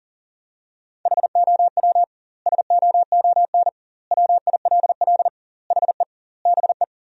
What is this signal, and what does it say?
Morse code, heard as a keyed tone, about 37 wpm, 700 Hz. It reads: HOW SOON WILL HE BE
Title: 2x Word-Spacing - Sentences from Top 400 Words - 40wpm